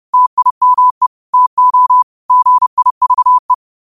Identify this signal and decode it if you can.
Morse code, heard as a keyed tone, about 30 wpm, 1000 Hz. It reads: TIME TO GIVE
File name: qotdfiles / 30wpm